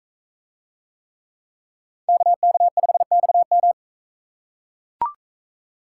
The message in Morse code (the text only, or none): KK5XM